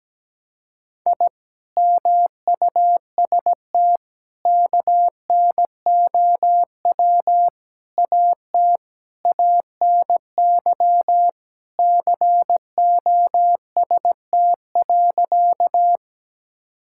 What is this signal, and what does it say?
Morse code, heard as a keyed tone, about 17 wpm, 700 Hz. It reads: I MUST KNOW AT ANY COST.
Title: Sentences from Top 1000 Words - Review Part 5 - 17wpm